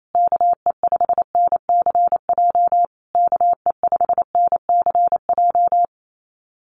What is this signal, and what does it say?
Morse code, heard as a keyed tone, about 28 wpm, 700 Hz. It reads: KE5NCJ KE5NCJ
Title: USA Call Signs Repeated 3x 28wpm